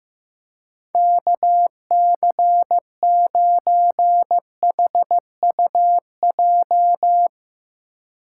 Morse code, heard as a keyed tone, about 15 wpm, 700 Hz. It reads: KC9HUJ
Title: USA Call Signs - 2x Character Spacing - 15wpm